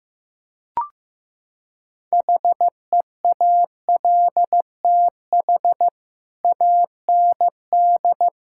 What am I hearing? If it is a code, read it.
Morse code, heard as a keyed tone, about 15 wpm, 700 Hz. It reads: HEALTH AND
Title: Sentences from Top 1000 Words - Review Part 6 - 15wpm